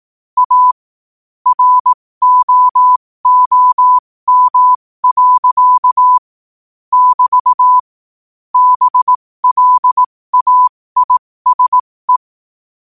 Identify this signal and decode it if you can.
Morse code, heard as a keyed tone, about 18 wpm, spaced as for 15 wpm, 1000 Hz. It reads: A ROOM. <BT> BLAISE
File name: qotdfiles / 15wpm